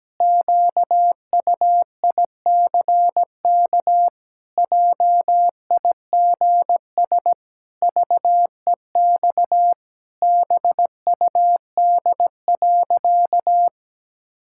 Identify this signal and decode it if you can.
Morse code, heard as a keyed tone, about 17 wpm, 700 Hz. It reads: QUICK JIGS VEX BUD.
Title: Pangram Challenge v2 - 17wpm